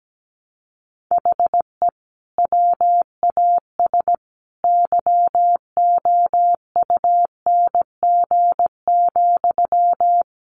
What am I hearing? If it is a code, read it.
Morse code, heard as a keyed tone, about 17 wpm, 700 Hz. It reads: HE WAS YOUNG,
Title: Sentences from Top 1100 Words 17wpm